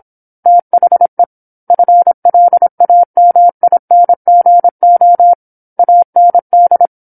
Morse code, heard as a keyed tone, about 26 wpm, 700 Hz. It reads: THE FLAMINGO AND
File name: 08-Chapter-08 26 WPM